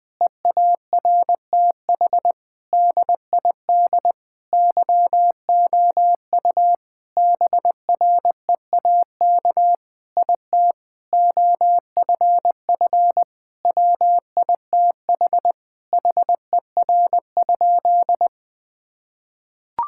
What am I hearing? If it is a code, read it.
Morse code, heard as a keyed tone, about 20 wpm, 700 Hz. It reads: EARTH DID YOU BREAK IT OFF WITH HER?
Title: Sentences from Top 1000 Words - Review Part 6 - 20wpm